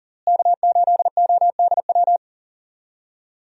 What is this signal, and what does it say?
Morse code, heard as a keyed tone, about 40 wpm, 700 Hz. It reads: K8ODW
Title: USA Call Signs - 5 Characters - Repeated 3x - 40wpm